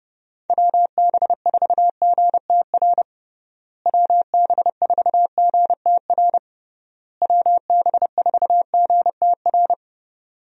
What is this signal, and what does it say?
Morse code, heard as a keyed tone, about 30 wpm, 700 Hz. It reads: WB4GTR WB4GTR WB4GTR
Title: USA Call Signs - 6 Characters - Repeated 3x - 30wpm